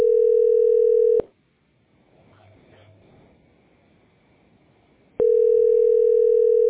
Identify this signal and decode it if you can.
Ringback tone (North America), 440 + 480 Hz.